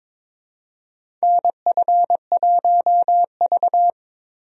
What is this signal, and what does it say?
Morse code, heard as a keyed tone, about 22 wpm, 700 Hz. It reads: NF1V
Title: USA Call Signs - 4 Characters - 22wpm